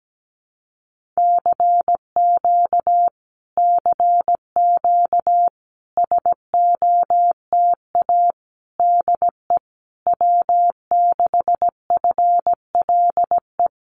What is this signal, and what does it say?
Morse code, heard as a keyed tone, about 17 wpm, 700 Hz. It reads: CQ CQ SOTA DE W6FLE
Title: Common QSO Phrases Encore 17wpm